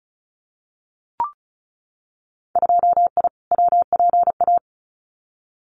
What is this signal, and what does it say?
Morse code, heard as a keyed tone, about 34 wpm, 700 Hz. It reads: 2I WPA
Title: Winter Field Day Exchanges 35wpm